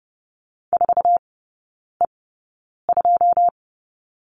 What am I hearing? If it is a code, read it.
Morse code, heard as a keyed tone, about 30 wpm, spaced as for 9 wpm, 700 Hz. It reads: 4E2
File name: 081 Lesson 14e - Number 2 - Three Characters 30wpm